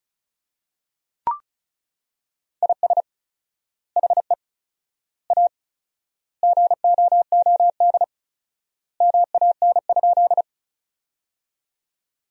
Morse code, heard as a keyed tone, about 36 wpm, 700 Hz. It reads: IS HE A GOOD MAN?
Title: 4x Word-Spacing - Sentences from Top 200 Words - 35wpm